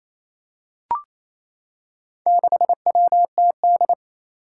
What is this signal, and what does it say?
Morse code, heard as a keyed tone, about 27 wpm, 700 Hz. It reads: 6WTD
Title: Four Challenging Letter-Number Combinations 28wpm